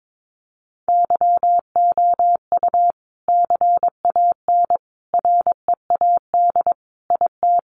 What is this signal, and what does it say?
Morse code, heard as a keyed tone, about 22 wpm, 700 Hz. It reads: YOU CAN READ IT
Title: Sentences from Top 600 Words 22wpm